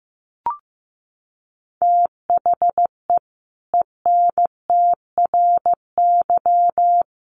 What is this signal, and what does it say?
Morse code, heard as a keyed tone, about 15 wpm, 700 Hz. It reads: THE ENTRY
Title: Sentences from Top 1500 Words 15wpm